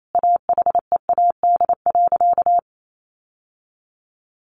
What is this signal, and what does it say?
Morse code, heard as a keyed tone, about 28 wpm, 700 Hz. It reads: AHEAD.
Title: Sentences from Top 1100 Words 28wpm